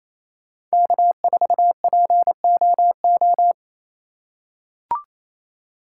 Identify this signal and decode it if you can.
Morse code, heard as a keyed tone, about 28 wpm, 700 Hz. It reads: K4POO
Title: USA Call Signs - 5 Characters - 2x Character Spacing - 28wpm